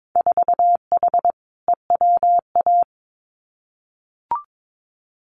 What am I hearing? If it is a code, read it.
Morse code, heard as a keyed tone, about 22 wpm, 700 Hz. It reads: 4H EWA
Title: Winter Field Day Exchanges Encore 22wpm